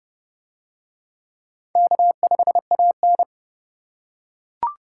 Morse code, heard as a keyed tone, about 31 wpm, 700 Hz. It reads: K5AN